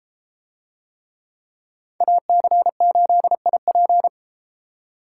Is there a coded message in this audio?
Yes: Morse code from 2.0 to 4.1 s, heard as a keyed tone, about 33 wpm, 700 Hz. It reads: AC8IP